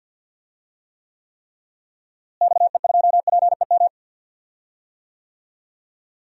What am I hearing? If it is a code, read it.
Morse code, heard as a keyed tone, about 46 wpm, 700 Hz. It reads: XE2PEM
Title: DXCC-100 Call Signs - 6 Characters - Repeated 3x - 50wpm